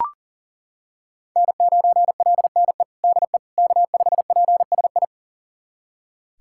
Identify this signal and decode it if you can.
Morse code, heard as a keyed tone, about 37 wpm, 700 Hz. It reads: N9LNE DE K5PSI
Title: USA Callsign DE Callsign - 5 Characters - 40wpm